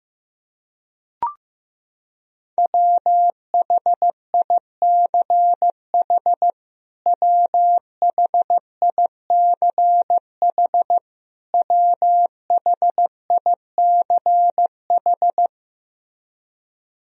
Morse code, heard as a keyed tone, about 15 wpm, 700 Hz. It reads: WHICH WHICH WHICH
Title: Top 100 Words Repeated 15wpm